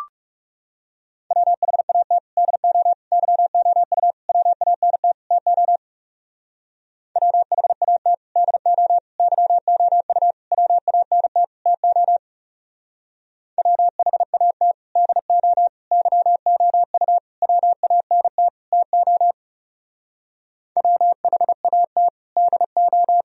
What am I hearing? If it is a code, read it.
Morse code, heard as a keyed tone, about 36 wpm, 700 Hz. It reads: WHAT DO YOU WANT TO WHAT DO YOU WANT TO WHAT DO YOU WANT TO WHAT DO
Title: Speed-Racing - Sets of 5 Words 30wpm